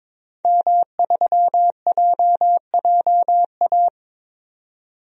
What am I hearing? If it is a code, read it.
Morse code, heard as a keyed tone, about 22 wpm, 700 Hz. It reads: M3JJA
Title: DXCC-100 Call Signs - 5 Characters - 22wpm